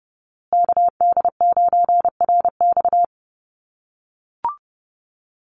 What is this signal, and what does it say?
Morse code, heard as a keyed tone, about 30 wpm, 700 Hz. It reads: KD9RX
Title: USA Call Signs Repeated 3x 30wpm